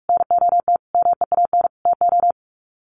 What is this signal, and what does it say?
Morse code, heard as a keyed tone, about 44 wpm, 700 Hz. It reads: NOT MEAN TO
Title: Sets of 5 Words 45wpm